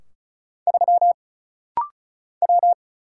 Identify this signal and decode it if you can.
Morse code, heard as a keyed tone, about 34 wpm, 700 Hz. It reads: 3 W
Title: Single Letter-Number Alt - Rapid-Fire 35wpm